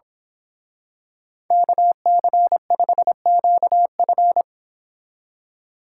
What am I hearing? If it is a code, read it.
Morse code, heard as a keyed tone, about 26 wpm, 700 Hz. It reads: KC5QF
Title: Speed Racing - USA 2x2 Call Signs 22wpm